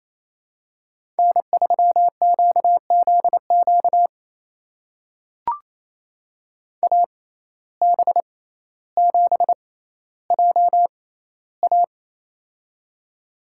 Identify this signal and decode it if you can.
Morse code, heard as a keyed tone, about 28 wpm, spaced as for 14 wpm, 700 Hz. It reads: N3QZQ AB7JA
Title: USA Call Signs - 5 Characters - 6x Character Spacing - 28wpm